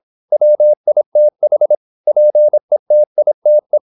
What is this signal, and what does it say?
Morse code, heard as a keyed tone, about 26 wpm, 600 Hz. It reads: WITH PETITE
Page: Never 26wpm | CW med Gnister